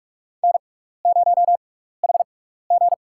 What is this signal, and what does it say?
Morse code, heard as a keyed tone, about 46 wpm, spaced as for 16 wpm, 700 Hz. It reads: N0HG
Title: USA Call Signs - 4 Characters - 6x Character Spacing - 45wpm